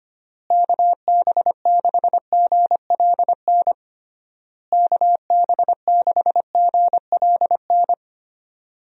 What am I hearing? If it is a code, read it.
Morse code, heard as a keyed tone, about 25 wpm, 700 Hz. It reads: KB6GLN KB6GLN
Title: USA Call Signs - 6 Characters - Repeated 3x - 25wpm